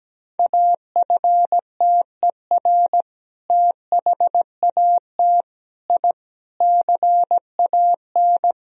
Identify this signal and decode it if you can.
Morse code, heard as a keyed tone, about 17 wpm, 700 Hz. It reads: AFTER THAT I CAN
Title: Sentences from Top 1000 Words - Review Part 2 - 17wpm